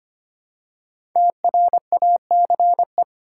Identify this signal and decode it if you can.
Morse code, heard as a keyed tone, about 25 wpm, 700 Hz. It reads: TRACE
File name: 5 Letter Words - First 3 Letters Given - 25wpm